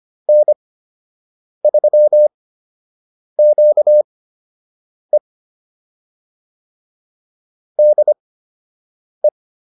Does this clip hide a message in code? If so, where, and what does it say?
Yes: Morse code from 0.3 to 9.3 s, heard as a keyed tone, about 25 wpm, spaced as for 7 wpm, 600 Hz. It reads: N3QE DE